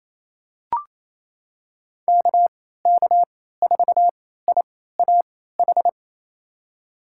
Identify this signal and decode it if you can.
Morse code, heard as a keyed tone, about 27 wpm, spaced as for 16 wpm, 700 Hz. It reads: KK4IAH
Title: USA Call Signs - 6 Characters - 3x Character Spacing - 28wpm